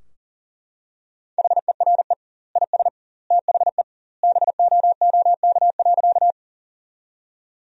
Morse code, heard as a keyed tone, about 37 wpm, 700 Hz. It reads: HERE IS THE BOOK.
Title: 2x Word-Spacing - Sentences from Top 300 Words - 40wpm